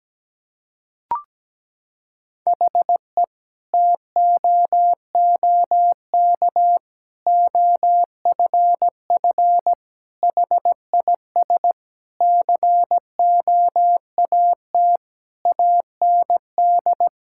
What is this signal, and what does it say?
Morse code, heard as a keyed tone, about 17 wpm, 700 Hz. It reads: HE TOOK OFF HIS COAT AND
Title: Sentences from Top 1600 Words 17wpm